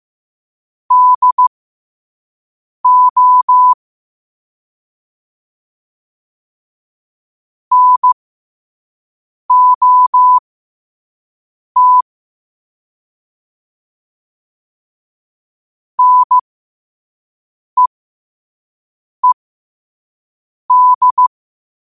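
Morse code, heard as a keyed tone, about 15 wpm, spaced as for 5 wpm, 1000 Hz. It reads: DO NOT NEED
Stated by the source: Morse code quote-of-the-day practice at 5 wpm.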